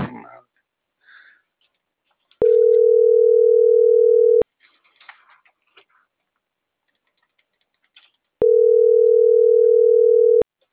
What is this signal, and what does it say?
Ringback tone (North America), 440 + 480 Hz.